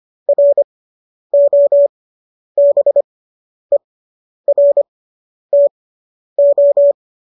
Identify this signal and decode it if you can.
Morse code, heard as a keyed tone, about 25 wpm, spaced as for 10 wpm, 570 Hz. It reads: ROBERTO